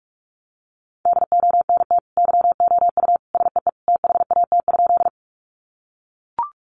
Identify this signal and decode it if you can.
Morse code, heard as a keyed tone, about 44 wpm, 700 Hz. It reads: DONT YOU SEE THAT?
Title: Sentences from Top 300 Words 45wpm